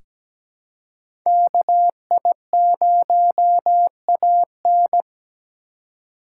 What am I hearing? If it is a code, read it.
Morse code, heard as a keyed tone, about 17 wpm, 700 Hz. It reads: KI0AN
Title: USA Call Signs - 5 Characters - 2x Character Spacing - 17wpm